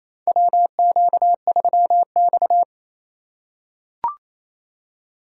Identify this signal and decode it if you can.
Morse code, heard as a keyed tone, about 28 wpm, 700 Hz. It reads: WQ3X
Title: USA Call Signs - 4 Characters - 4x Character Spacing - 28wpm